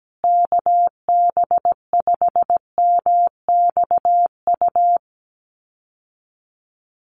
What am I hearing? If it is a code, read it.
Morse code, heard as a keyed tone, about 17 wpm, 700 Hz. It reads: KB5MXU
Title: USA Call Signs Repeated 2x 17wpm